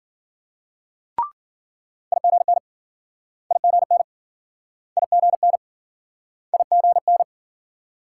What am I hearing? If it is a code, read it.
Morse code, heard as a keyed tone, about 52 wpm, 700 Hz. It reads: IGN IGN IGN IGN